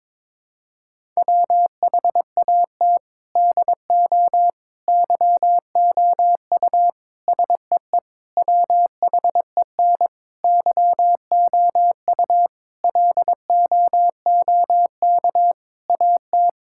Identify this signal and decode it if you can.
Morse code, heard as a keyed tone, about 22 wpm, 700 Hz. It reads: WHAT DO YOU SEE WHEN YOU LOOK AT
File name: practice / 22wpm / Sentences with 4-Word Phrases - 22wpm